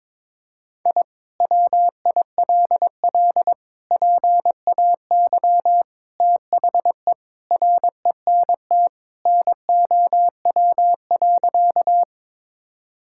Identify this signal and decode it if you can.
Morse code, heard as a keyed tone, about 22 wpm, 700 Hz. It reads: I WILL PAY THE RENT NOW.
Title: Sentences from Top 1000 Words - Review Part 5 - 22wpm